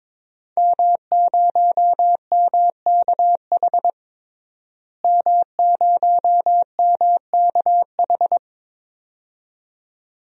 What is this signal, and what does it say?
Morse code, heard as a keyed tone, about 22 wpm, 700 Hz. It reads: M0MKH M0MKH
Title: DXCC-100 Call Signs - 5 Characters - Repeated 2x - 22wpm